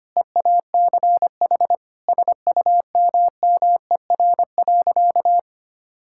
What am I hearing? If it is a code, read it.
Morse code, heard as a keyed tone, about 25 wpm, 700 Hz. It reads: EACH SUMMER.
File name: practice / 25wpm / Difficult Sentences with Hints - 25wpm